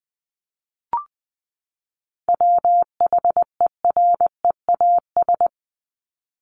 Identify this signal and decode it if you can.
Morse code, heard as a keyed tone, about 20 wpm, 700 Hz. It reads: WHEREAS
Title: Top 1501-1750 Words 20wpm